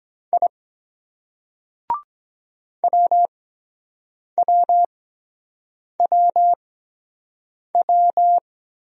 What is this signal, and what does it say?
Morse code, heard as a keyed tone, about 22 wpm, spaced as for 7 wpm, 700 Hz. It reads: I WWWW